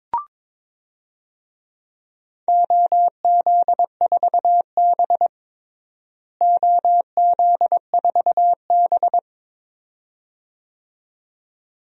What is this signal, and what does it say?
Morse code, heard as a keyed tone, about 21 wpm, 700 Hz. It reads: OZ4B OZ4B
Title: DXCC-100 Call Signs - 4 Characters - Repeated 2x - 22wpm